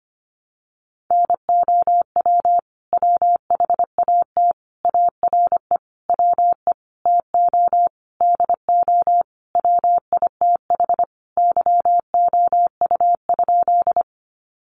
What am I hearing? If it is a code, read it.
Morse code, heard as a keyed tone, about 25 wpm, 700 Hz. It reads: NOW WHAT ARE WE TO DO WITH YOU?
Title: Sentences from Top 100 Words 25wpm